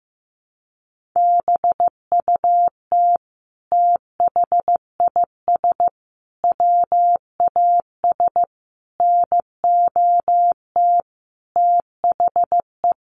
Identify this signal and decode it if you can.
Morse code, heard as a keyed tone, about 15 wpm, 700 Hz. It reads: BUT THIS WAS NOT THE
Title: Sentences from Top 700 Words 15wpm